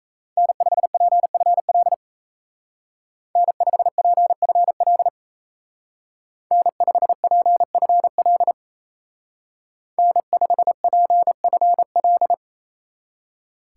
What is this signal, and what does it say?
Morse code, heard as a keyed tone, about 34 wpm, 700 Hz. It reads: N5PFL N5PFL N5PFL N5PFL
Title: Speed Racing - USA 1x3 Call Signs 28wpm